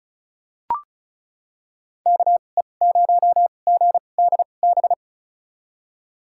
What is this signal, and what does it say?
Morse code, heard as a keyed tone, about 34 wpm, spaced as for 25 wpm, 700 Hz. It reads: KE0GDB
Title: USA Call Signs - 6 Characters - 2x Character Spacing - 35wpm